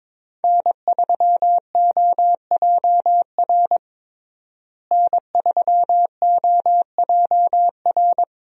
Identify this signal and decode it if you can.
Morse code, heard as a keyed tone, about 21 wpm, 700 Hz. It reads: N3OJR N3OJR
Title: USA Call Signs - 5 Characters - Repeated 2x - 22wpm